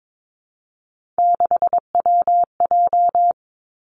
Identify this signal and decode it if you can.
Morse code, heard as a keyed tone, about 22 wpm, 700 Hz. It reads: 6WJ
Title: Three Challenging Letter-Number Combinations 22wpm